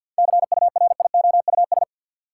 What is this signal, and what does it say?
Morse code, heard as a keyed tone, about 52 wpm, 700 Hz. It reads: CURIOUS
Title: Seven Letter Words - 50wpm